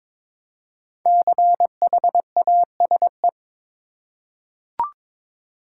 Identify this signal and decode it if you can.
Morse code, heard as a keyed tone, about 22 wpm, 700 Hz. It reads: CHASE
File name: 5 Letter Words - First 3 Letters Given - 22wpm